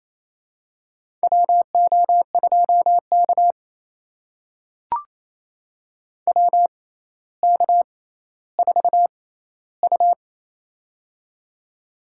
Morse code, heard as a keyed tone, about 28 wpm, spaced as for 14 wpm, 700 Hz. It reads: WO2K WK4U
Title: USA Call Signs - 6x Character Spacing - 28wpm